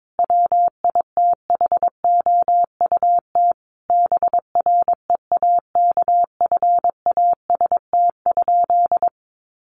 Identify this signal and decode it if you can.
Morse code, heard as a keyed tone, about 22 wpm, 700 Hz. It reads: WITHOUT BREAKFAST?
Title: Sentences from Top 1500 Words 22wpm